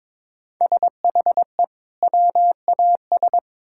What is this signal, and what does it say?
Morse code, heard as a keyed tone, about 22 wpm, 700 Hz. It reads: SHE WAS
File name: Build A Phrase - 2-Word Phrases - 22wpm